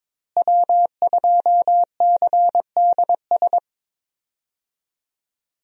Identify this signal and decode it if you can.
Morse code, heard as a keyed tone, about 22 wpm, 700 Hz. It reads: W2CDS
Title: USA Call Signs - 5 Characters - Repeated 3x - 22wpm